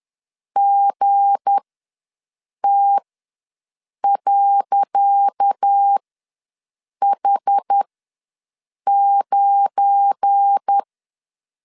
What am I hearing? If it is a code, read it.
Morse code, heard as a keyed tone, about 10 wpm, spaced as for 6 wpm, 800 Hz. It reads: GT.H9
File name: LEZIONI / 10 SIGNS / 10C DOT ws 10wpm 4213